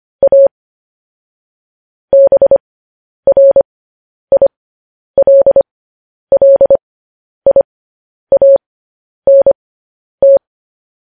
Morse code, heard as a keyed tone, about 25 wpm, spaced as for 10 wpm, 550 Hz. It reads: A BRILLIANT
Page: news headlines in morse code at 10 wpm